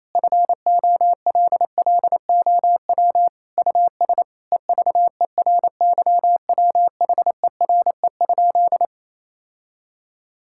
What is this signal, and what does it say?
Morse code, heard as a keyed tone, about 28 wpm, 700 Hz. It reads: FOLLOW US EVERYWHERE?
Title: Sentences from Top 1600 Words 28wpm